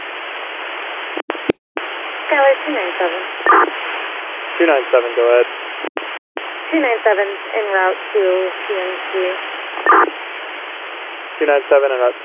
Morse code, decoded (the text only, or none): none